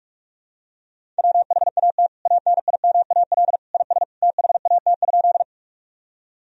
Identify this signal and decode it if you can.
Morse code, heard as a keyed tone, about 46 wpm, 700 Hz. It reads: WHAT ANIMAL IS THAT?
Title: Sentences from Top 1000 Words - Review Part 1 - 45wpm